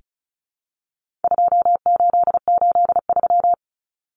Morse code, heard as a keyed tone, about 34 wpm, 700 Hz. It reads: 2883
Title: Common Cut Numbers - 4-Digit 35wpm